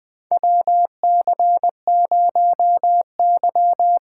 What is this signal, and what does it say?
Morse code, heard as a keyed tone, about 20 wpm, 700 Hz. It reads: WC0Y
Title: 2023 SOTA Chaser Call Signs - 20wpm